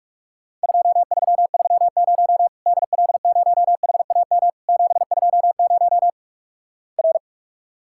Morse code, heard as a keyed tone, about 48 wpm, 700 Hz. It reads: 2330 DL0HAM 720 R
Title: Worked All Europe DX Contest - QTC Exchange - 45wpm